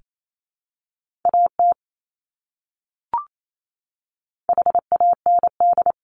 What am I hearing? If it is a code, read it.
Morse code, heard as a keyed tone, about 28 wpm, 700 Hz. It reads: AT HAND
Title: Top 300 Words Repeated 28wpm